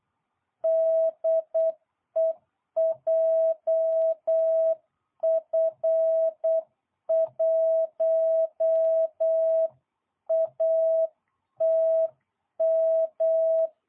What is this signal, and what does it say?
Morse code, heard as a keyed tone, about 8 wpm, 660 Hz. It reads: DE JF1ATM